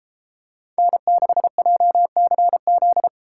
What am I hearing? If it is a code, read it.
Morse code, heard as a keyed tone, about 32 wpm, 700 Hz. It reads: N6JCZ